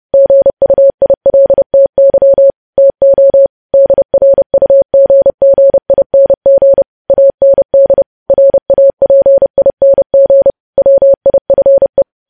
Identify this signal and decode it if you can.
Morse code, heard as a keyed tone, about 30 wpm, 550 Hz. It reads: GUILTY TO DRUGGING AND RAPING WIFE